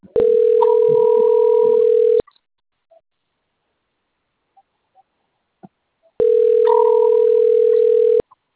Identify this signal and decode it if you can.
Ringback tone (North America), 440 + 480 Hz.
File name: out- - - - - .91709